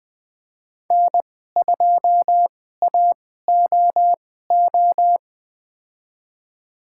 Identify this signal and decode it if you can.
Morse code, heard as a keyed tone, about 20 wpm, spaced as for 14 wpm, 700 Hz. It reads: N2AOO
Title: USA Call Signs - 5 Characters - 2x Character Spacing - 20wpm